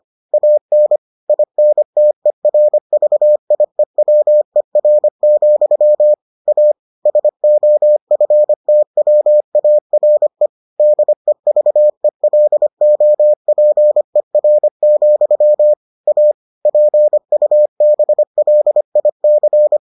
Morse code, heard as a keyed tone, about 25 wpm, 600 Hz. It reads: AN INTERVIEWER, A SOFTWARE DEVELOPER, A PUBLIC